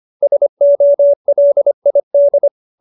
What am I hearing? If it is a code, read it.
Morse code, heard as a keyed tone, about 25 wpm, 590 Hz. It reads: SOLID
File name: QSO306 25